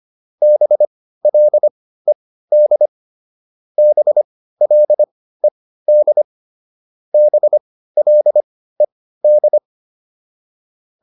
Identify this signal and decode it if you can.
Morse code, heard as a keyed tone, about 25 wpm, spaced as for 15 wpm, 600 Hz. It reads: BLED BLED BLED